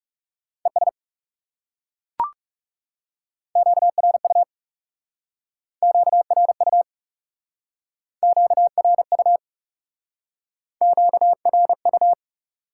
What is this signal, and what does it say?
Morse code, heard as a keyed tone, about 40 wpm, 700 Hz. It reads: ES QRU QRU QRU QRU